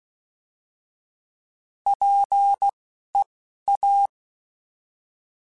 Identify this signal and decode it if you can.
Morse code, heard as a keyed tone, about 16 wpm, spaced as for 11 wpm, 800 Hz. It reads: PEA
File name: Alphabetical / Alphabetical mp3s / Ps @10